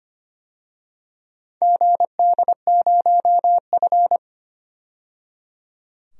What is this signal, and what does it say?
Morse code, heard as a keyed tone, about 25 wpm, 700 Hz. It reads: GD0F
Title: DXCC-100 Call Signs - 4 Characters - 25wpm